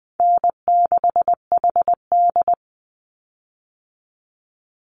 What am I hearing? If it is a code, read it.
Morse code, heard as a keyed tone, about 20 wpm, 700 Hz. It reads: N6HD
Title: Call Signs Repeated 2x - 20wpm